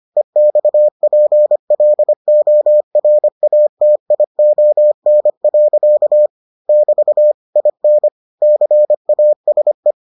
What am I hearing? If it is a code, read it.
Morse code, heard as a keyed tone, about 25 wpm, 600 Hz. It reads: EXPLORATION. <BT> IN CASE